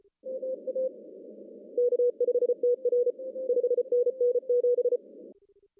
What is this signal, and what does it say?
Morse code, heard as a keyed tone, about 32 wpm, 470 Hz. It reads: MA K5TR 5NN 7
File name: CW / cqww-cw-k5tr